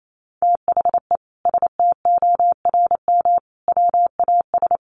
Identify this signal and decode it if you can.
Morse code, heard as a keyed tone, about 28 wpm, 700 Hz. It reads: THE STORM WAS
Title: Sentences from Top 1500 Words 28wpm